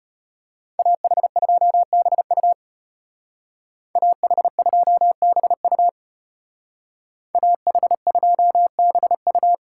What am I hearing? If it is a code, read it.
Morse code, heard as a keyed tone, about 30 wpm, 700 Hz. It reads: AH2BU AH2BU AH2BU
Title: Speed Racing - USA 2x2 Call Signs 25wpm